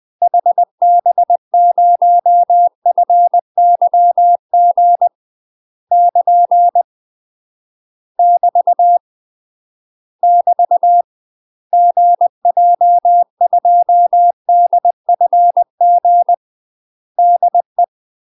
Morse code, heard as a keyed tone, about 20 wpm, 700 Hz. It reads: HB0FYG <KN> <BT> <BT> GJ2DFG DE
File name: UK MASTER QSOs TRAINING / 20WPM / UK MASTER QSOs 0029